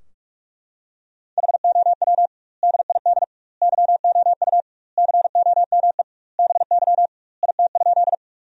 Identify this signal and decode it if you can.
Morse code, heard as a keyed tone, about 41 wpm, 700 Hz. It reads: HOW DID YOU COME BY IT?
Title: 2x Word-Spacing - Sentences from Top 100 Words - 45wpm